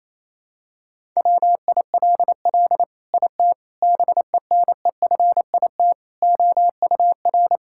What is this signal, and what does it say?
Morse code, heard as a keyed tone, about 28 wpm, 700 Hz. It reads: WILL IT BENEFIT OUR
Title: Sentences from Top 1000 Words - Review Part 6 - 28wpm